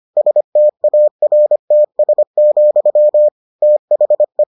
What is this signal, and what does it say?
Morse code, heard as a keyed tone, about 25 wpm, 600 Hz. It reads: STARTS, THE